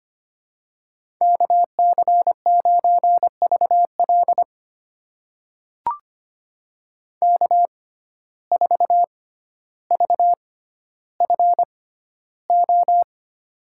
Morse code, heard as a keyed tone, about 25 wpm, spaced as for 13 wpm, 700 Hz. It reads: KC9VL K4VFO
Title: USA Call Signs - 5 Characters - 6x Character Spacing - 25wpm